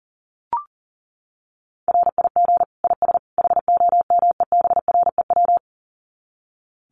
Morse code, heard as a keyed tone, about 40 wpm, 700 Hz. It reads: RIG IS HOMEBREW
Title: Common QSO Phrases 40wpm